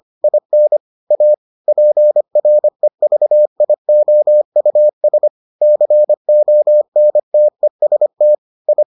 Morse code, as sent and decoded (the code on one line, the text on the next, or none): .. -. / .- / .--. .-. ...- .. --- ..- ... / -.-. --- -. - . ... - / ..
IN A PREVIOUS CONTEST I